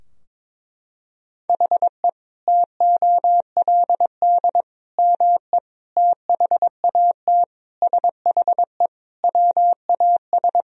Morse code, heard as a keyed tone, about 22 wpm, 700 Hz. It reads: HE TOLD ME THAT SHE WAS
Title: Sentences from Top 1000 Words - Review Part 2 - 22wpm